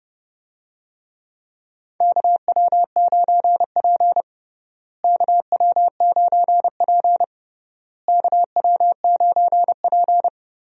Morse code, heard as a keyed tone, about 30 wpm, 700 Hz. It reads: KW9P KW9P KW9P
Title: USA Call Signs - 4 Characters - Repeated 3x - 30wpm